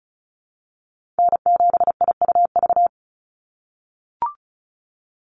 Morse code, heard as a keyed tone, about 34 wpm, 700 Hz. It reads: N7IUV
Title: USA Call Signs 35wpm